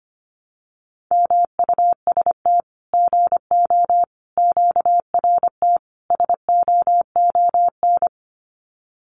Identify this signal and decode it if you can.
Morse code, heard as a keyed tone, about 25 wpm, 700 Hz. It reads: MUST GO QRT SOON
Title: Common QSO Phrases 25wpm